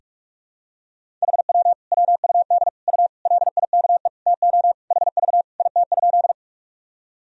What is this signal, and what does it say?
Morse code, heard as a keyed tone, about 46 wpm, 700 Hz. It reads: HW WUD U LIKE TO HV IT?
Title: Sentences from Top 100 Words and Abbreviations 45wpm